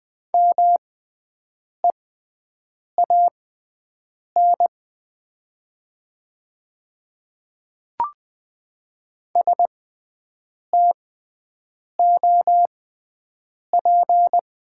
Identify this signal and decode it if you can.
Morse code, heard as a keyed tone, about 20 wpm, spaced as for 7 wpm, 700 Hz. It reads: MEAN STOP
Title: 6x Character Spacing - Top 500 Words - 20wpm